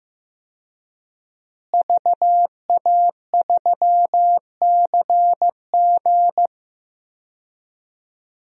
Morse code, heard as a keyed tone, about 15 wpm, 700 Hz. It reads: VA3CG
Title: DXCC-100 Call Signs - 5 Characters - Repeated 2x - 15wpm